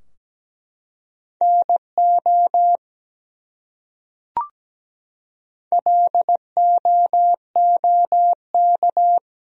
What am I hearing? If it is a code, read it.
Morse code, heard as a keyed tone, about 17 wpm, 700 Hz. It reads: NO LOOK
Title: Top 100 Words Repeated 17wpm